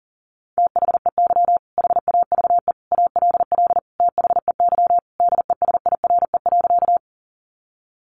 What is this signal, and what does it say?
Morse code, heard as a keyed tone, about 37 wpm, 700 Hz. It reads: THEY HAVE ALL THEY DESIRE.
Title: Sentences from Top 1100 Words 40wpm